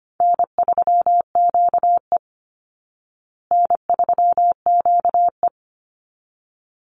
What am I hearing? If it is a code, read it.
Morse code, heard as a keyed tone, about 25 wpm, 700 Hz. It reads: N3QE N3QE
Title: Easy Call Signs Repeated 2x - 25wpm